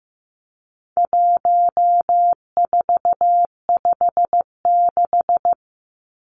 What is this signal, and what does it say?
Morse code, heard as a keyed tone, about 15 wpm, 700 Hz. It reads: 1456
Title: Numbers - 4-Digit 15wpm